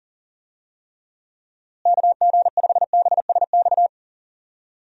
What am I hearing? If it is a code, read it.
Morse code, heard as a keyed tone, about 40 wpm, 700 Hz. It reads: KG5BSX
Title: USA Call Signs - 6 Characters - Repeated 3x - 40wpm